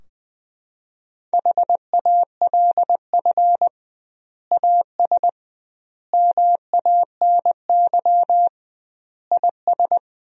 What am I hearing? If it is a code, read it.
Morse code, heard as a keyed tone, about 21 wpm, 700 Hz. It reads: HALF AS MANY IS
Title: 2x Word-Spacing - Sentences from Top 400 Words - 20wpm